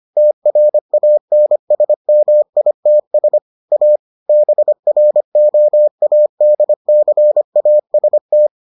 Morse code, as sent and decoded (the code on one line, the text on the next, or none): - .-. .- -. ... -- .. - ... / .- / -... .-. --- .- -.. -.-. .- ... -
TRANSMITS A BROADCAST